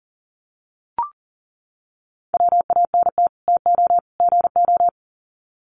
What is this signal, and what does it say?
Morse code, heard as a keyed tone, about 37 wpm, 700 Hz. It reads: WANT TO GO
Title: Sets of 3 Words 40wpm